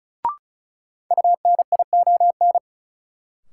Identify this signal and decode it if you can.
Morse code, heard as a keyed tone, about 36 wpm, 700 Hz. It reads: UNION
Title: Sentences from Top 1400 Words - New Word Drill Alt - 35wpm